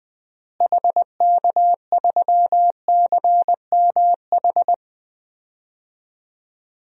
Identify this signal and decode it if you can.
Morse code, heard as a keyed tone, about 20 wpm, 700 Hz. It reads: HK3CMH
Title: DXCC-100 Call Signs - 6 Characters - Repeated 3x - 20wpm